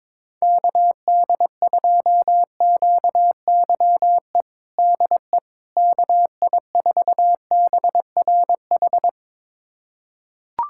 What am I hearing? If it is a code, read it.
Morse code, heard as a keyed tone, about 21 wpm, 700 Hz. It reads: KD2QYE DE KI4BRH
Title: USA Callsign DE Callsign 22wpm